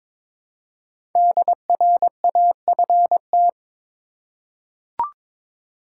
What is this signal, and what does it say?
Morse code, heard as a keyed tone, about 22 wpm, 700 Hz. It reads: DRAFT
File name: 5 Letter Words - First 3 Letters Given - 22wpm